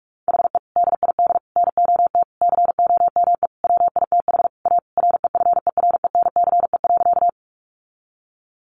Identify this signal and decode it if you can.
Morse code, heard as a keyed tone, about 44 wpm, 700 Hz. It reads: HE DID NOT COME WITH A REFERENCE.
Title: Sentences from Top 1200 Words 45wpm